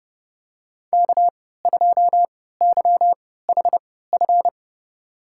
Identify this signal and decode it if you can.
Morse code, heard as a keyed tone, about 29 wpm, spaced as for 17 wpm, 700 Hz. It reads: K2YHF
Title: USA Call Signs - 3x Character Spacing - 30wpm